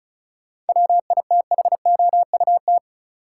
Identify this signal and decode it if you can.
Morse code, heard as a keyed tone, about 36 wpm, 700 Hz. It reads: WITHOUT
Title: Sentences from Top 600 Words - New Word Drill - 35wpm